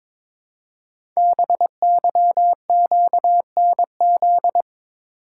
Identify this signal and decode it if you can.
Morse code, heard as a keyed tone, about 22 wpm, 690 Hz. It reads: BYQNZ
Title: Code Groups - 5 Letters - 22wpm